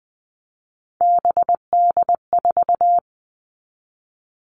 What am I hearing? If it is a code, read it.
Morse code, heard as a keyed tone, about 20 wpm, 700 Hz. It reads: BD4
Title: Callsign Prefixes for Top 100 DXCC - 20wpm